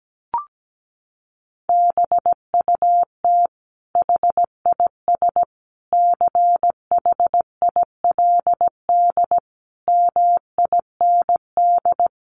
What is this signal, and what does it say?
Morse code, heard as a keyed tone, about 17 wpm, 700 Hz. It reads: BUT HIS CHILD MIND
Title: Sentences from Top 800 Words 17wpm